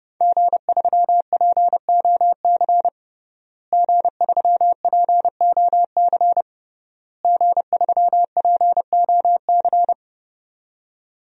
Morse code, heard as a keyed tone, about 30 wpm, 700 Hz. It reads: G3POC G3POC G3POC
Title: DXCC-100 Call Signs - 5 Characters - Repeated 3x - 30wpm